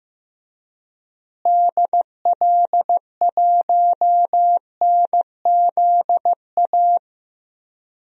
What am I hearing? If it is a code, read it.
Morse code, heard as a keyed tone, about 15 wpm, 700 Hz. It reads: DL1NZA
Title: DXCC-100 Call Signs - 6 Characters - Repeated 3x - 15wpm